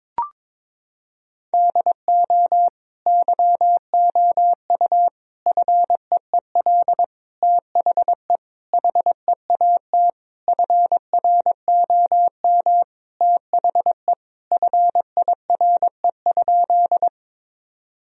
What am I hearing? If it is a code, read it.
Morse code, heard as a keyed tone, about 21 wpm, 700 Hz. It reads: DO YOU FEEL THE HEAT FROM THE FIRE?
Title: Sentences from Top 1000 Words - Review Part 3 - 22wpm